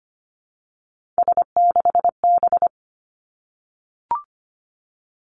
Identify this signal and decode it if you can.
Morse code, heard as a keyed tone, about 25 wpm, 700 Hz. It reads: S6B
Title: Three Challenging Letter-Number Combinations 25wpm